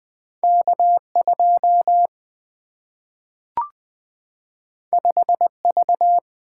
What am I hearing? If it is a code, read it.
Morse code, heard as a keyed tone, about 20 wpm, 700 Hz. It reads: K 2 5 V
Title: Single-Letter Single-Number 20wpm